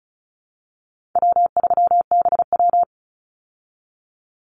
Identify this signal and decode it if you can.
Morse code, heard as a keyed tone, about 34 wpm, 700 Hz. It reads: W3BW
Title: Easy Call Signs Repeated 3x - 35wpm